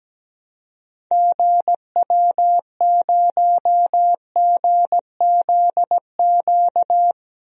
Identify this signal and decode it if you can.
Morse code, heard as a keyed tone, about 17 wpm, 700 Hz. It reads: GW0GZQ
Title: DXCC-100 Call Signs - 6 Characters - 17wpm